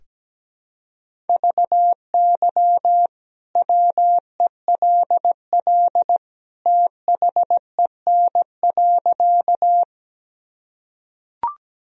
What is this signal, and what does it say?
Morse code, heard as a keyed tone, about 17 wpm, 700 Hz. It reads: VY WELL THEN.
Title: Sentences from Top 200 Words and Abbreviations 17wpm